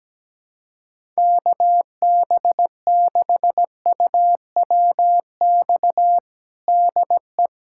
Morse code, heard as a keyed tone, about 17 wpm, 700 Hz. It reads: KB6UWX DE
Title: USA Callsign DE Callsign 17wpm